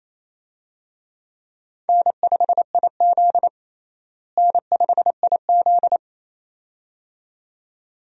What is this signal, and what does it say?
Morse code, heard as a keyed tone, about 27 wpm, 700 Hz. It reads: N5IZ N5IZ
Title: USA Call Signs - 4 Characters - Repeated 2x - 28wpm